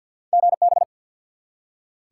Morse code, heard as a keyed tone, about 47 wpm, 700 Hz. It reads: GB
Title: Top 180 QSO Elements 50wpm